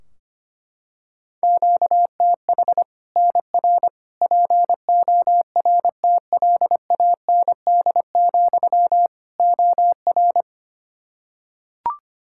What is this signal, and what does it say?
Morse code, heard as a keyed tone, about 25 wpm, 700 Hz. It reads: QTH NR PORTLAND, OR
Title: Practice QSO - Exchange 1 - Beginning - 25wpm